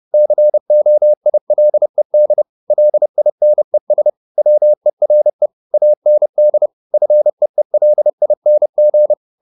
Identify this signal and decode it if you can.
Morse code, heard as a keyed tone, about 30 wpm, 600 Hz. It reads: COILED LINES WERE AND FEELING
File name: the old man and the sea / the old man and the sea 1 30wpm